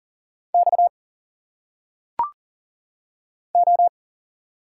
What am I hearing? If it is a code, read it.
Morse code, heard as a keyed tone, about 40 wpm, 700 Hz. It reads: X O
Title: Single Letters - Phonetic Pronunciation 40wpm